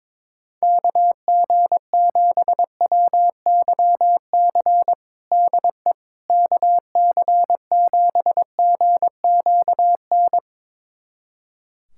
Morse code, heard as a keyed tone, about 21 wpm, 700 Hz. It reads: KG7WYC DE KC7GQN
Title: USA Callsign DE Callsign 22wpm